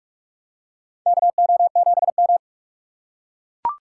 Morse code, heard as a keyed tone, about 44 wpm, 700 Hz. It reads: KO7M
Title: USA Call Signs - 4 Characters - 2x Character Spacing - 45wpm